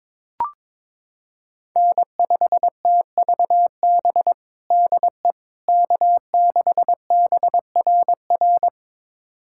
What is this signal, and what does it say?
Morse code, heard as a keyed tone, about 22 wpm, 700 Hz. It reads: N5TVB DE K6BRR
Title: USA Callsign DE Callsign - 5 Characters - 22wpm